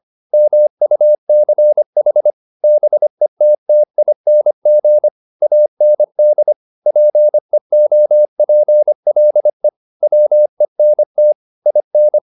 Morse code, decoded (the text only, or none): MUCH BETTING AND PEOPLE WENT IN